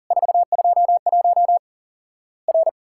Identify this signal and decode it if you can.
Morse code, heard as a keyed tone, about 44 wpm, 700 Hz. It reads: 421 R
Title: Worked All Europe DX Contest - QTC Exchange - 40wpm